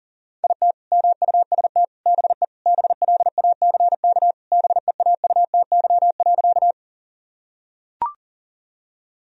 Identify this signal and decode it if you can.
Morse code, heard as a keyed tone, about 40 wpm, 700 Hz. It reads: IT MUST BE BLACK BEAUTY.
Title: Sentences from Top 1000 Words - Review Part 2 - 40wpm